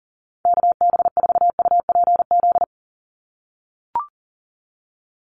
Morse code, heard as a keyed tone, about 40 wpm, 700 Hz. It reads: KB4UPZ
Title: USA Call Signs 40wpm